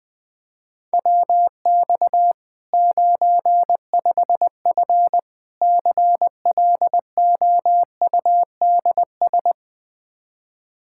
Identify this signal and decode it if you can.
Morse code, heard as a keyed tone, about 21 wpm, 700 Hz. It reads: WX 95F CLOUDS
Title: K1USN SST QSO Practice 20wpm